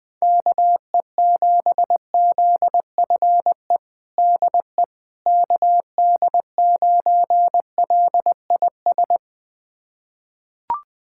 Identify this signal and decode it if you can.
Morse code, heard as a keyed tone, about 20 wpm, 700 Hz. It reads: KE7ZFE DE KD9LIS
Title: USA Callsign DE Callsign 20wpm